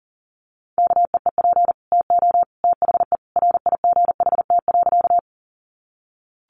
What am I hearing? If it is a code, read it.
Morse code, heard as a keyed tone, about 40 wpm, 700 Hz. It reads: KEEP TO THE RIGHT.
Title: Sentences from Top 300 Words 40wpm